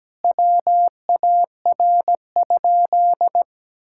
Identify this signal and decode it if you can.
Morse code, heard as a keyed tone, about 17 wpm, 700 Hz. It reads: WAR?
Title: Sentences from Top 1000 Words - Review Part 5 - 17wpm